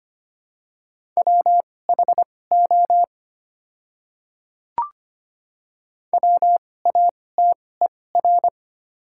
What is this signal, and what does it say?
Morse code, heard as a keyed tone, about 25 wpm, spaced as for 18 wpm, 690 Hz. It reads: WHO WATER
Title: 2x Character Spacing - Top 100 Words - 25wpm